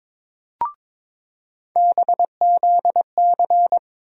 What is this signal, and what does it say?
Morse code, heard as a keyed tone, about 21 wpm, 700 Hz. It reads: BZC
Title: USA Call Sign Postfixes - 3 Letters 22wpm